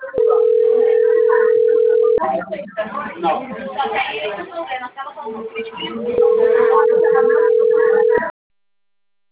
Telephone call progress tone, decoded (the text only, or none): ringback tone